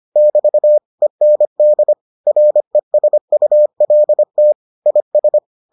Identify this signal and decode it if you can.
Morse code, heard as a keyed tone, about 25 wpm, 600 Hz. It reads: <BT> END RESULT IS